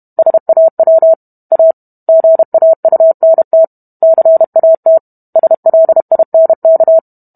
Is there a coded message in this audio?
Yes: Morse code from 0.2 to 7.0 s, heard as a keyed tone, about 32 wpm, 650 Hz. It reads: SAW A GAUNT CAT SLINK